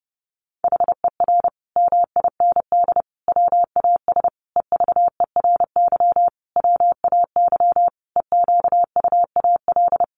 Morse code, heard as a keyed tone, about 28 wpm, 700 Hz. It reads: HER MIND WAS EVERY WAY EQUAL
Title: Sentences from Top 1400 Words 30wpm